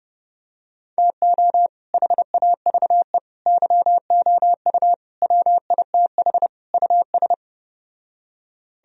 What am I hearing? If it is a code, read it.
Morse code, heard as a keyed tone, about 30 wpm, 700 Hz. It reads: TO HAVE YOU WITH US